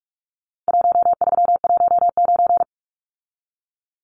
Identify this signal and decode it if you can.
Morse code, heard as a keyed tone, about 44 wpm, 700 Hz. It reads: 1319
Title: Numbers - Contest Style - 4-Digit 45wpm